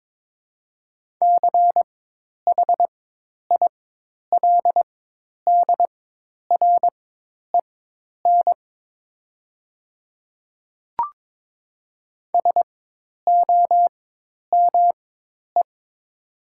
Morse code, heard as a keyed tone, about 22 wpm, spaced as for 10 wpm, 700 Hz. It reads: CHILDREN SOME
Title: 4x Character Spacing - Top 300 Words - 22wpm